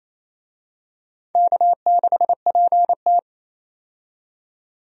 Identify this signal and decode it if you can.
Morse code, heard as a keyed tone, about 29 wpm, 690 Hz. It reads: K6PT
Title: USA Call Signs - 4 Characters - 28wpm